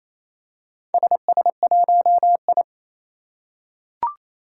Morse code, heard as a keyed tone, about 29 wpm, 700 Hz. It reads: SS1I
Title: Four Challenging Letter-Number Combinations 28wpm